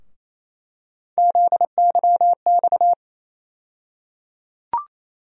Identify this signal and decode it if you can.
Morse code, heard as a keyed tone, about 28 wpm, 700 Hz. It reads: ZYX
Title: USA Call Sign Postfixes - 3 Letters 28wpm